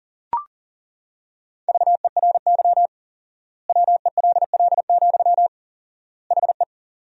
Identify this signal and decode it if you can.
Morse code, heard as a keyed tone, about 40 wpm, 700 Hz. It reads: VERY WELL, HE
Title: 4x Word-Spacing - Sentences from Top 200 Words - 40wpm